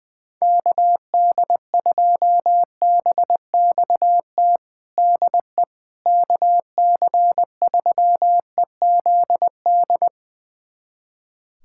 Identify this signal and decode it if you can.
Morse code, heard as a keyed tone, about 20 wpm, 700 Hz. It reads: KD2BXT DE KC3EZD
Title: USA Callsign DE Callsign 20wpm